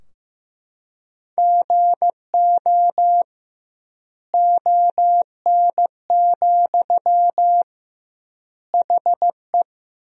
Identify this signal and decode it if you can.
Morse code, heard as a keyed tone, about 15 wpm, 700 Hz. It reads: GO ON, HE
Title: 2x Word-Spacing - Sentences from Top 100 Words - 15wpm